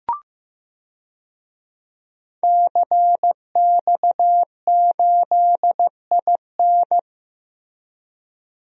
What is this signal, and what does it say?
Morse code, heard as a keyed tone, about 15 wpm, 700 Hz. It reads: CX8IN
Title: DXCC-100 Call Signs - 5 Characters - Repeated 2x - 15wpm